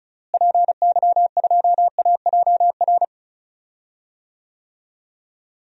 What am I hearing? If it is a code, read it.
Morse code, heard as a keyed tone, about 36 wpm, 700 Hz. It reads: PY2AJR
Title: DXCC-100 Call Signs - 6 Characters - Repeated 3x - 35wpm